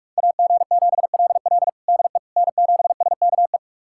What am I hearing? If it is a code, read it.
Morse code, heard as a keyed tone, about 43 wpm, 700 Hz. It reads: AG7LL DE N7SKE
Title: USA Callsign DE Callsign - 5 Characters - 45wpm